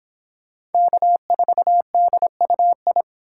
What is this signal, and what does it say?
Morse code, heard as a keyed tone, about 25 wpm, 700 Hz. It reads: K4DUI